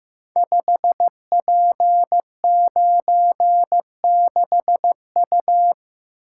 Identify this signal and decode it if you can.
Morse code, heard as a keyed tone, about 15 wpm, 700 Hz. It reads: 5P96U
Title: Code Groups - 5 Letter:Numbers - 15wpm